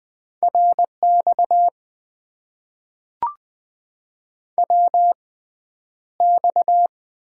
Morse code, heard as a keyed tone, about 20 wpm, spaced as for 10 wpm, 700 Hz. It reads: RX WX
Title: Top 90 QSO Elements - 6x Character Spacing - 20wpm